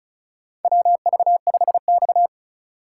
Morse code, heard as a keyed tone, about 35 wpm, 700 Hz. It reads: WV5X
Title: USA Call Signs - 4 Characters - 2x Character Spacing - 35wpm